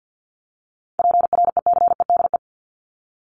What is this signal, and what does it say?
Morse code, heard as a keyed tone, about 49 wpm, 700 Hz. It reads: PRECEDE
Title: ING Suffix - Top 100 Words with Base Words 50wpm